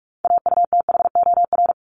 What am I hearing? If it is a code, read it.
Morse code, heard as a keyed tone, about 44 wpm, 700 Hz. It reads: AUTHOR
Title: Top 1001-1250 Words 45wpm